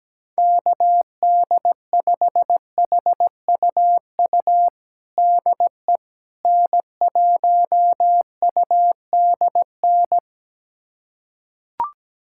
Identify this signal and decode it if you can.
Morse code, heard as a keyed tone, about 17 wpm, 700 Hz. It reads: KD5HUU DE N1UDN
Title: USA Callsign DE Callsign 17wpm